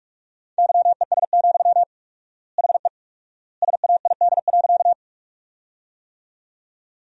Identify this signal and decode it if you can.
Morse code, heard as a keyed tone, about 44 wpm, 700 Hz. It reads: YES, HE SAID.
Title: 4x Word-Spacing - Sentences from Top 500 Words - 45wpm